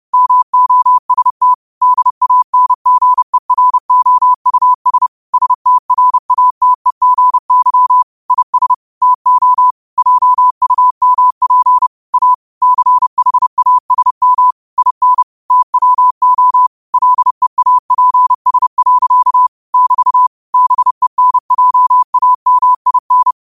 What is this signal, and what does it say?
Morse code, heard as a keyed tone, about 30 wpm, 1000 Hz. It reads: MOST DANGEROUS STRATEGY IS TO JUMP A CHASM IN TWO LEAPS. <BT> BENJAMIN
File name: qotdfiles / 30wpm